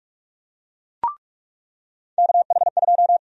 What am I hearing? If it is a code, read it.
Morse code, heard as a keyed tone, about 46 wpm, 700 Hz. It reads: KH2
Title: USA Call Sign Prefixes - 45wpm